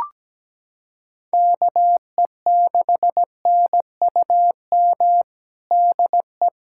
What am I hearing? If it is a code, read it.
Morse code, heard as a keyed tone, about 17 wpm, 700 Hz. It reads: KE6NUM DE
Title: USA Callsign DE Callsign - 6 Characters - 17wpm